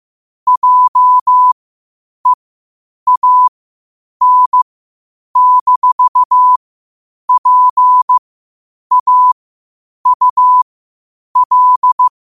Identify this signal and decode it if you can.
Morse code, heard as a keyed tone, about 15 wpm, spaced as for 8 wpm, 1000 Hz. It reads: JEAN-PAUL
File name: qotdfiles / 8wpm